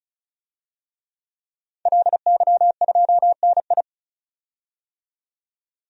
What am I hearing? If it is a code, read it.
Morse code, heard as a keyed tone, about 35 wpm, 700 Hz. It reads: LY2NI
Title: DXCC-100 Call Signs - 5 Characters - 35wpm